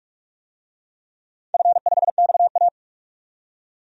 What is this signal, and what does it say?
Morse code, heard as a keyed tone, about 46 wpm, 700 Hz. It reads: F5XA
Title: DXCC-100 Call Signs - 4 Characters - 45wpm